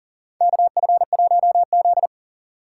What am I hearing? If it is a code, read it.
Morse code, heard as a keyed tone, about 40 wpm, 700 Hz. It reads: KF1Z
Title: USA Call Signs - 4 Characters - 2x Character Spacing - 40wpm